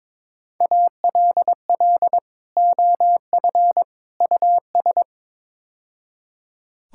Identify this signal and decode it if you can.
Morse code, heard as a keyed tone, about 22 wpm, 700 Hz. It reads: ALL OF US
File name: practice / 22wpm / Build A Phrase - 5-Word Phrases - 22wpm